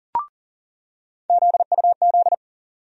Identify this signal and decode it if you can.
Morse code, heard as a keyed tone, about 40 wpm, 700 Hz. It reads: ZUZ
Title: USA Call Sign Postfixes - 3 Letters 40wpm